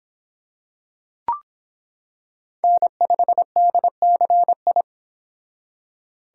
Morse code, heard as a keyed tone, about 25 wpm, 700 Hz. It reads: N5DCI